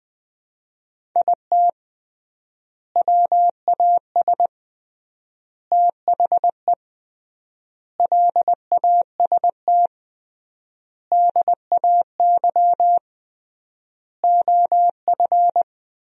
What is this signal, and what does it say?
Morse code, heard as a keyed tone, about 20 wpm, 700 Hz. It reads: IT WAS THE LAST DAY OF
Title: 3x Word-Spacing - Sentences from Top 300 Words - 20wpm